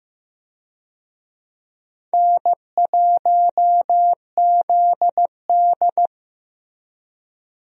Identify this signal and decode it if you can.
Morse code, heard as a keyed tone, about 15 wpm, 700 Hz. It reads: N1ZD
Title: USA Call Signs - 4 Characters - Repeated 2x - 15wpm